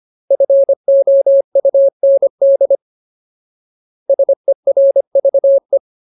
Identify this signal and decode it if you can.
Morse code, heard as a keyed tone, about 25 wpm, 550 Hz. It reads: FOUND SERVE
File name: Ultimate Word List / 25WPM Ultimate